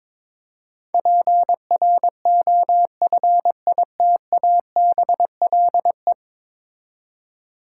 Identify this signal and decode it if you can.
Morse code, heard as a keyed tone, about 22 wpm, 700 Hz. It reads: PROFITABLE
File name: practice / 22wpm / 10 Letter Words - First 3 Letters Given - 22wpm